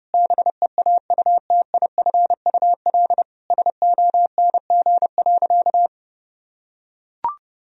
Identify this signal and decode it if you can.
Morse code, heard as a keyed tone, about 30 wpm, 700 Hz. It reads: BEAUTIFUL SONG.
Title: Sentences from Top 1000 Words - Review Part 2 - 30wpm